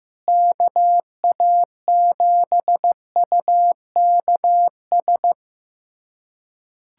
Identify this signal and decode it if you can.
Morse code, heard as a keyed tone, about 15 wpm, 700 Hz. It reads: KA7UKS
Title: USA Callsign DE Callsign - 6 Characters - 15wpm